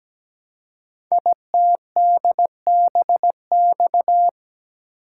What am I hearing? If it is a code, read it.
Morse code, heard as a keyed tone, about 17 wpm, 700 Hz. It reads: ITDBX
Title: Code Groups - 5 Letters - 17wpm